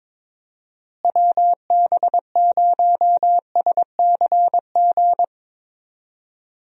Morse code, heard as a keyed tone, about 21 wpm, 700 Hz. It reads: WB0SCG
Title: USA Call Signs - 6 Characters - Repeated 2x - 22wpm